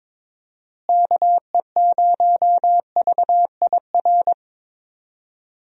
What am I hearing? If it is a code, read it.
Morse code, heard as a keyed tone, about 22 wpm, 700 Hz. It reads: KE0VIR
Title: USA Call Signs - 6 Characters - Repeated 3x - 22wpm